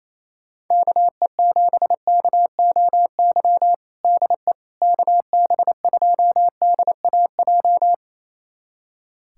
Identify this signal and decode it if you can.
Morse code, heard as a keyed tone, about 28 wpm, 700 Hz. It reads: KE7KOY DE KB2DAJ
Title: USA Callsign DE Callsign - 6 Characters - 28wpm